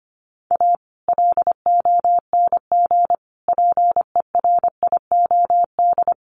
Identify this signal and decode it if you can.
Morse code, heard as a keyed tone, about 25 wpm, 700 Hz. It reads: A LONG PERIOD
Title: Sets of 5 Words 25wpm